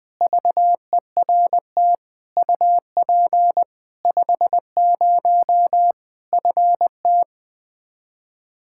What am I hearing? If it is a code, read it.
Morse code, heard as a keyed tone, about 21 wpm, 700 Hz. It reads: VERT UP 50 FT
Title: Practice QSO - Exchange 2 - Info - 20wpm